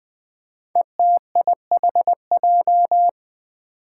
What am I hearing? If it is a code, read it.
Morse code, heard as a keyed tone, about 20 wpm, 700 Hz. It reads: ETIHJ
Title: Five Challenging Letter-Number Combinations 20wpm